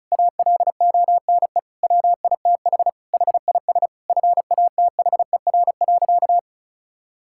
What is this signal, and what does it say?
Morse code, heard as a keyed tone, about 34 wpm, 700 Hz. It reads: ALONE WITH HIS FATHER.
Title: Sentences from Top 1000 Words - Review Part 5 - 35wpm